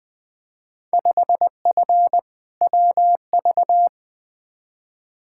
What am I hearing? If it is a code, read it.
Morse code, heard as a keyed tone, about 20 wpm, 700 Hz. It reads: 5F WV
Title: Field Day Exchanges Encore - 20wpm